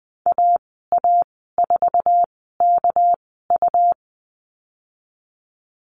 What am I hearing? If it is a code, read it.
Morse code, heard as a keyed tone, about 20 wpm, spaced as for 15 wpm, 700 Hz. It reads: AA4KU
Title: 219 Lesson 35g - Number 0 - Call Signs 20wpm